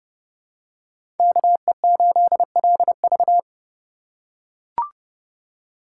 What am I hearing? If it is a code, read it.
Morse code, heard as a keyed tone, about 30 wpm, 700 Hz. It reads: KE8LV
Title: USA Call Signs - 4x Character Spacing - 30wpm